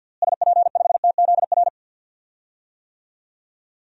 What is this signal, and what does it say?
Morse code, heard as a keyed tone, about 46 wpm, 700 Hz. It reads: SP5TZR
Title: DXCC-100 Call Signs - 6 Characters - Repeated 3x - 50wpm